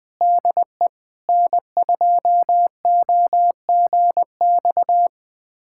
Morse code, heard as a keyed tone, about 21 wpm, 700 Hz. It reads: DE N2OGX
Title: USA Callsign DE Callsign - 5 Characters - 20wpm